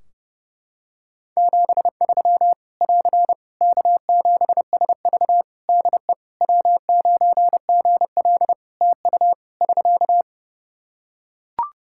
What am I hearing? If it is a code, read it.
Morse code, heard as a keyed tone, about 30 wpm, 700 Hz. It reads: 73 <AR> K7SV DE W9GL TU <SK>
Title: Practice QSO - Exchange 4 - Ending - 30wpm